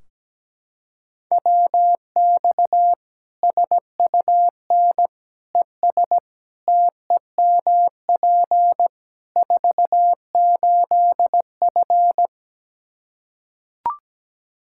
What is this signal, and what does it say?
Morse code, heard as a keyed tone, about 17 wpm, 700 Hz. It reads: WX SUN ES TEMP 48F
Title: K1USN SST QSO Practice 17wpm